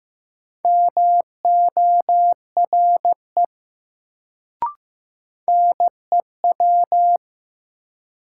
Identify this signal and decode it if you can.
Morse code, heard as a keyed tone, about 15 wpm, 700 Hz. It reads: MORE NEW
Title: Sets of 2 Words - New Word Drill - 15wpm